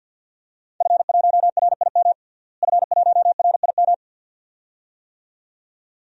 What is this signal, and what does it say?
Morse code, heard as a keyed tone, about 50 wpm, 700 Hz. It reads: F1RIM F1RIM
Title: DXCC-100 Call Signs - 5 Characters - Repeated 2x - 50wpm